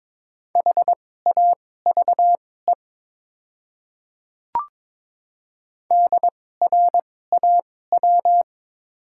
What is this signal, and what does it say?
Morse code, heard as a keyed tone, about 22 wpm, spaced as for 16 wpm, 690 Hz. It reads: HAVE DRAW
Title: 2x Character Spacing - Top 300 Words - 22wpm